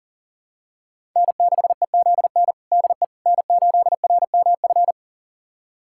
Morse code, heard as a keyed tone, about 40 wpm, 700 Hz. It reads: N6EZN DE N8RMF
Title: USA Callsign DE Callsign - 5 Characters - 40wpm